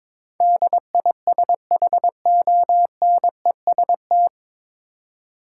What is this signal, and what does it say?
Morse code, heard as a keyed tone, about 22 wpm, 700 Hz. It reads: DISHONEST
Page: Morse Code Ninja practice 22wpm